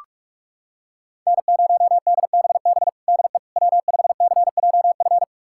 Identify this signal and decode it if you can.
Morse code, heard as a keyed tone, about 42 wpm, 700 Hz. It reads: N0DBB DE W5CJF
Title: USA Callsign DE Callsign - 5 Characters - 45wpm